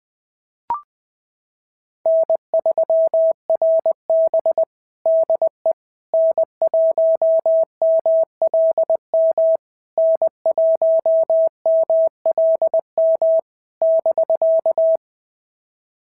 Morse code, heard as a keyed tone, about 20 wpm, 650 Hz. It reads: N3RB DE N1MLM N1MLM <BK>
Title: SOTA QSO Practice 20wpm